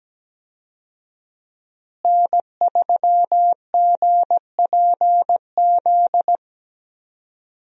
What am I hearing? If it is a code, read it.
Morse code, heard as a keyed tone, about 17 wpm, 700 Hz. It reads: N3GPZ
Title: USA Call Signs - 5 Characters - Repeated 2x - 17wpm